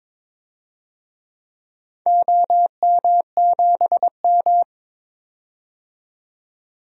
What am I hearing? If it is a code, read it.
Morse code, heard as a keyed tone, about 22 wpm, 700 Hz. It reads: OM7M
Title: DXCC-100 Call Signs - 4 Characters - 22wpm